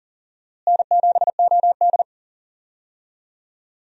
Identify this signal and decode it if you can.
Morse code, heard as a keyed tone, about 40 wpm, 700 Hz. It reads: N7OD
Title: USA Call Signs - 4 Characters - Repeated 3x - 40wpm